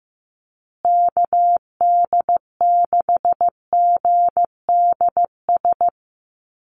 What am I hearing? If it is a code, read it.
Morse code, heard as a keyed tone, about 15 wpm, 700 Hz. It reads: KD6GDS
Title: USA Call Signs Repeated 3x 15wpm